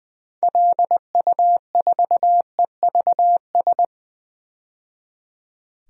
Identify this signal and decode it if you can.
Morse code, heard as a keyed tone, about 20 wpm, 700 Hz. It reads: LU4EVS
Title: DXCC-100 Call Signs - 6 Characters - 20wpm